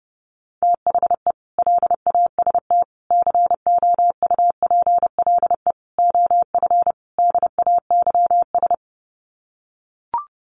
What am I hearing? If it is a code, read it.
Morse code, heard as a keyed tone, about 30 wpm, 700 Hz. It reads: THE LAST COUPLE OF DAYS
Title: Sets of 5 Words Encore 30wpm